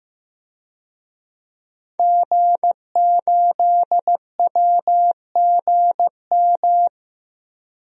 Morse code, heard as a keyed tone, about 15 wpm, 700 Hz. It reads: G8WGM
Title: DXCC-100 Call Signs - 5 Characters - Repeated 3x - 15wpm